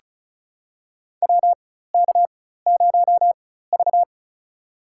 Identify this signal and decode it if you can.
Morse code, heard as a keyed tone, about 36 wpm, spaced as for 16 wpm, 700 Hz. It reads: WK0V
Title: USA Call Signs - 4 Characters - 4x Character Spacing - 35wpm